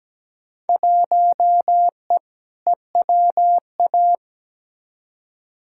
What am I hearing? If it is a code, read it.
Morse code, heard as a keyed tone, about 17 wpm, 700 Hz. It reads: 1E EWA
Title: Field Day Exchanges - 17wpm